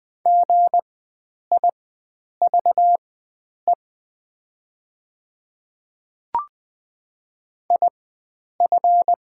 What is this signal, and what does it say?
Morse code, heard as a keyed tone, about 20 wpm, spaced as for 9 wpm, 700 Hz. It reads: GIVE IF
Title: 4x Character Spacing - Top 300 Words - 20wpm